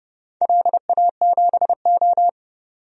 Morse code, heard as a keyed tone, about 30 wpm, 700 Hz. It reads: LA7O
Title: DXCC-100 Call Signs - 4 Characters - 30wpm